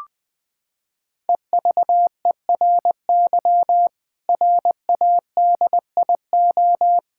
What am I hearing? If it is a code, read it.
Morse code, heard as a keyed tone, about 20 wpm, 700 Hz. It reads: EVERY RADIO
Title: Sentences from Top 1000 Words - Review Part 6 - 20wpm